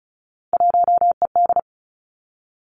Morse code, heard as a keyed tone, about 36 wpm, 700 Hz. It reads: 1ED
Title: Three Challenging Letter-Number Combinations 35wpm